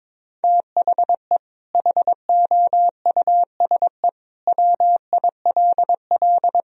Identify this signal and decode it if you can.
Morse code, heard as a keyed tone, about 22 wpm, 700 Hz. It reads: THE HOUSE WILL
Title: Sentences from Top 1000 Words - Review Part 2 - 22wpm